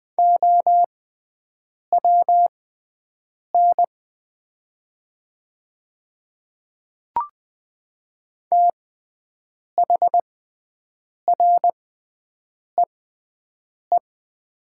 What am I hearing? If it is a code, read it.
Morse code, heard as a keyed tone, about 20 wpm, spaced as for 7 wpm, 700 Hz. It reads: OWN THREE
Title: 6x Character Spacing - Top 500 Words - 20wpm